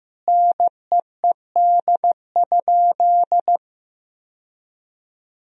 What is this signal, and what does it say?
Morse code, heard as a keyed tone, about 15 wpm, 700 Hz. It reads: NEED?
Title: Sentences from Top 1000 Words - Review Part 4 - 15wpm